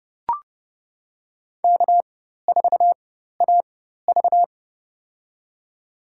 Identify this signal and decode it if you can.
Morse code, heard as a keyed tone, about 29 wpm, spaced as for 14 wpm, 700 Hz. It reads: K4AV
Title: USA Call Signs - 4 Characters - 4x Character Spacing - 30wpm